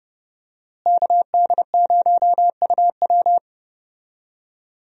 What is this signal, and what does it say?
Morse code, heard as a keyed tone, about 30 wpm, 700 Hz. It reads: KD0UW
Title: USA Call Signs - 5 Characters - Repeated 2x - 30wpm